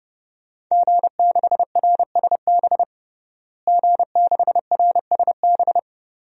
Morse code, heard as a keyed tone, about 32 wpm, 700 Hz. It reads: G6RSB G6RSB
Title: DXCC-100 Call Signs - 5 Characters - Repeated 3x - 30wpm